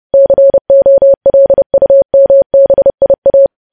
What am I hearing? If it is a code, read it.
Morse code, heard as a keyed tone, about 30 wpm, 550 Hz. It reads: COLUMBIA
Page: news headlines in morse code at 30 wpm